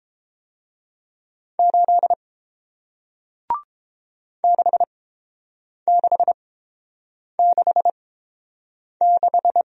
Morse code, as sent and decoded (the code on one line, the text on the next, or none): ---.. / -.... -.... -.... -....
8 6666